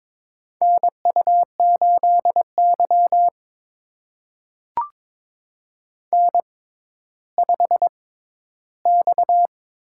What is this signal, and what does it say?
Morse code, heard as a keyed tone, about 22 wpm, spaced as for 12 wpm, 700 Hz. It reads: NU8Y N5X
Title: USA Call Signs - 4 Characters - 6x Character Spacing - 22wpm